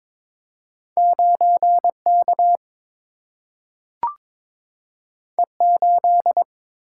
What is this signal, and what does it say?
Morse code, heard as a keyed tone, about 22 wpm, 700 Hz. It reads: 9 K E 8
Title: Single-Letter Single-Number 22wpm